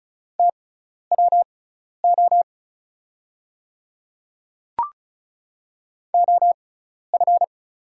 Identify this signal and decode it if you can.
Morse code, heard as a keyed tone, about 36 wpm, spaced as for 12 wpm, 700 Hz. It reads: TWO OF
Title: 6x Character Spacing - Top 100 Words - 35wpm